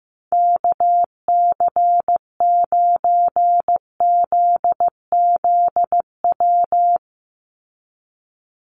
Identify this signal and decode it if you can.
Morse code, heard as a keyed tone, about 15 wpm, 700 Hz. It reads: KC9ZZW
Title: USA Call Signs 15wpm